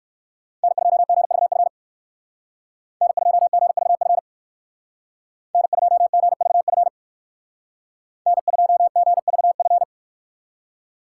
Morse code, heard as a keyed tone, about 58 wpm, 700 Hz. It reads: N2GVF N2GVF N2GVF N2GVF